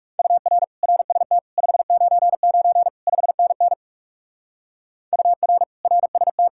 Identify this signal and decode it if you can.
Morse code, heard as a keyed tone, about 36 wpm, 700 Hz. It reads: UR RST 599 5NN UR RST
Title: Speed-Racing Common QSO Phrases - 30wpm